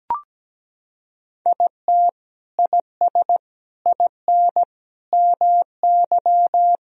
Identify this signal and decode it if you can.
Morse code, heard as a keyed tone, about 17 wpm, 700 Hz. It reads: IT IS IN MY
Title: Sentences from Top 1000 Words - Review Part 1 - 17wpm